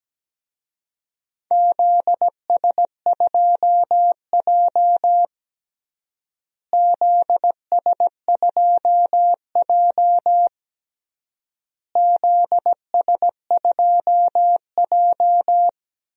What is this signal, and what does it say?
Morse code, heard as a keyed tone, about 17 wpm, 700 Hz. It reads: ZS2J ZS2J ZS2J
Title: DXCC-100 Call Signs - 4 Characters - Repeated 3x - 17wpm